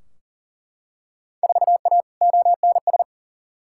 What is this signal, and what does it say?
Morse code, heard as a keyed tone, about 40 wpm, 700 Hz. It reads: 4A ONS
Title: Field Day Exchanges - 40wpm